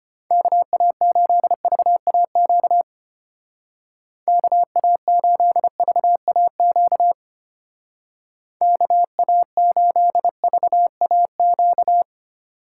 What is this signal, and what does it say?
Morse code, heard as a keyed tone, about 33 wpm, 700 Hz. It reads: KA8VAQ KA8VAQ KA8VAQ